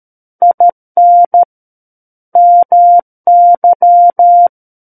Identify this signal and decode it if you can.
Morse code, heard as a keyed tone, about 13 wpm, 700 Hz. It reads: IN MY
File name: Book One / 04-Chapter 4 - 13 WPM